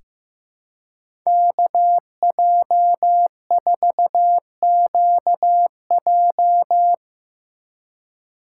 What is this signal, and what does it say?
Morse code, heard as a keyed tone, about 15 wpm, 700 Hz. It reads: KJ4QJ
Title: USA Call Signs - 5 Characters - 4x Character Spacing - 15wpm